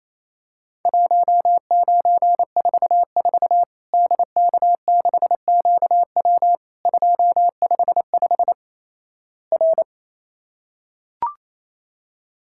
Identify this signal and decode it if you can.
Morse code, heard as a keyed tone, about 28 wpm, 700 Hz. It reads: 1944 DK6QW 255 R
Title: Worked All Europe DX Contest - QTC Exchange - 28wpm